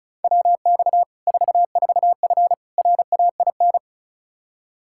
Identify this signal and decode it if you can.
Morse code, heard as a keyed tone, about 34 wpm, 700 Hz. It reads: WX 44F RAIN
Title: Practice QSO - Exchange 2 - Info - 35wpm